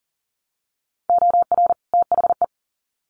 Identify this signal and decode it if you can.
Morse code, heard as a keyed tone, about 40 wpm, 700 Hz. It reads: OR THE
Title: Sets of 2 Words 40wpm